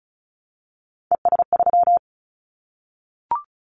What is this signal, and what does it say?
Morse code, heard as a keyed tone, about 36 wpm, 700 Hz. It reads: ES3
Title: Callsign Prefixes for Top 50 DXCC - 35wpm